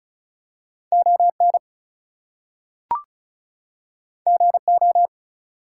Morse code, heard as a keyed tone, about 36 wpm, 700 Hz. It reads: ON GO
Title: Two Letter Words - 35wpm